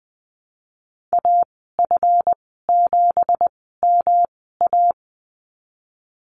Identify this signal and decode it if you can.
Morse code, heard as a keyed tone, about 20 wpm, spaced as for 14 wpm, 700 Hz. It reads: AF7MA
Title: 179 Lesson 29g - Slash - Call Signs 20wpm